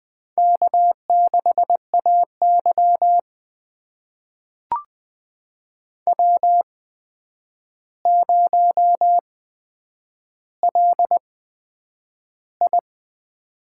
Morse code, heard as a keyed tone, about 20 wpm, spaced as for 9 wpm, 700 Hz. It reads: K6AY W0LI
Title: USA Call Signs - 4 Characters - 8x Character Spacing - 20wpm